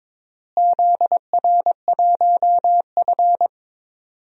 Morse code, heard as a keyed tone, about 22 wpm, 700 Hz. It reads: ZR1F
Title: DXCC-100 Call Signs - 4 Characters - Repeated 2x - 22wpm